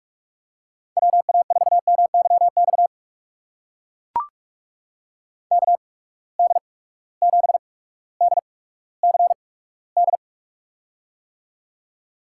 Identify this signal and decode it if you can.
Morse code, heard as a keyed tone, about 44 wpm, spaced as for 19 wpm, 700 Hz. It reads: WA4MYX KD7DCD
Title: USA Call Signs - 6 Characters - 8x Character Spacing - 45wpm